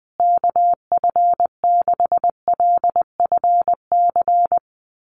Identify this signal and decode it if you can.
Morse code, heard as a keyed tone, about 20 wpm, 700 Hz. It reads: KF6LFC
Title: USA Call Signs Repeated 3x 20wpm